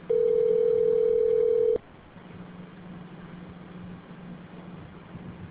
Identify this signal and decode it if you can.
Ringback tone (North America), 440 + 480 Hz.